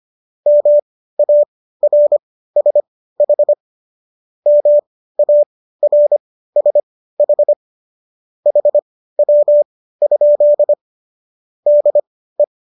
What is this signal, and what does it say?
Morse code, heard as a keyed tone, about 25 wpm, spaced as for 15 wpm, 590 Hz. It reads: MARSH MARSH HW? DE
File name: qso211 15